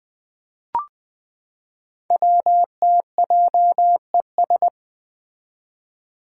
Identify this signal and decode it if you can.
Morse code, heard as a keyed tone, about 20 wpm, 700 Hz. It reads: WTJES
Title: Code Groups - 5 Letters - 20wpm